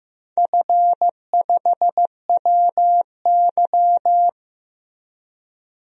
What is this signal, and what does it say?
Morse code, heard as a keyed tone, about 15 wpm, 700 Hz. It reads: F5WY
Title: DXCC-100 Call Signs - 4 Characters - Repeated 3x - 15wpm